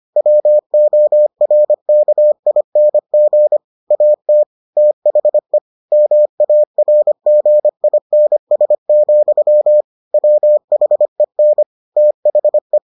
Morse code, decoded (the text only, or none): WORKING AT THE MARGINS, WHEN THE